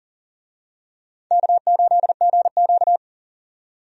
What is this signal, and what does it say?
Morse code, heard as a keyed tone, about 38 wpm, 700 Hz. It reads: K8GQ
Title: USA Call Signs - 4 Characters - 40wpm